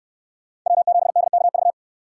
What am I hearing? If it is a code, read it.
Morse code, heard as a keyed tone, about 64 wpm, 700 Hz. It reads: W7DKV